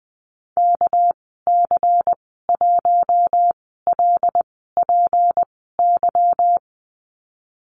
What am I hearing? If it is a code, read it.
Morse code, heard as a keyed tone, about 20 wpm, spaced as for 15 wpm, 700 Hz. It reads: KC1LPY
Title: 231 Lesson 37g - Letter x - Call Signs 20wpm